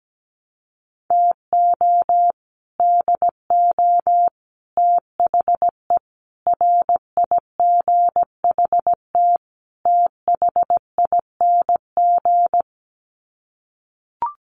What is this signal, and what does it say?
Morse code, heard as a keyed tone, about 17 wpm, 700 Hz. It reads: TO DO THE RIGHT THING
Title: Sets of 5 Words 17wpm